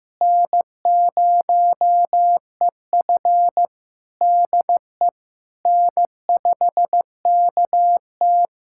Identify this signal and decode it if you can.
Morse code, heard as a keyed tone, about 16 wpm, 700 Hz. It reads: N0EF DE N5KT
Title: Practice QSO - Exchange 4 - Ending - 15wpm